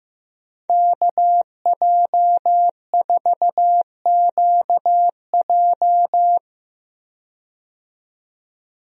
Morse code, heard as a keyed tone, about 15 wpm, 700 Hz. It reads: KJ4QJ
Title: USA Call Signs - 5 Characters - 15wpm